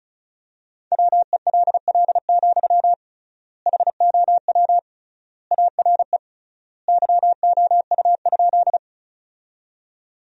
Morse code, heard as a keyed tone, about 34 wpm, 700 Hz. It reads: WELL, HOW ARE YOU?
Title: 3x Word-Spacing - Sentences from Top 200 Words - 35wpm